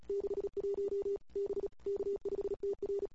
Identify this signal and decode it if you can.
Morse code, heard as a keyed tone, about 34 wpm, 410 Hz. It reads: 61 B K5TR
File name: CW / 2008-ss-cw-k5tr-op-n5rz -21:00